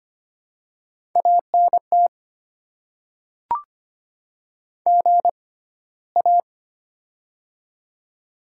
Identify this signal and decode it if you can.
Morse code, heard as a keyed tone, about 25 wpm, spaced as for 15 wpm, 700 Hz. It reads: ANT GA
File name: Top 20 QSO Elements - 6x Character Spacing - 25wpm